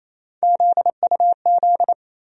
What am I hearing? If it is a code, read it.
Morse code, heard as a keyed tone, about 28 wpm, 700 Hz. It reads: ZUZ
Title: USA Call Sign Postfixes - 3 Letters 28wpm